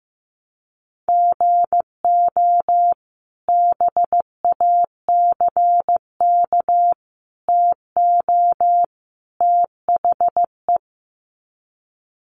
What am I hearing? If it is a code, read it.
Morse code, heard as a keyed tone, about 14 wpm, 700 Hz. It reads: GO BACK TO THE
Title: Sets of 4 Words 15wpm